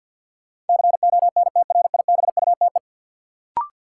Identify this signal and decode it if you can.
Morse code, heard as a keyed tone, about 52 wpm, 700 Hz. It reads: CONTRIBUTE
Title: Sentences from Top 1400 Words - New Word Drill - 50wpm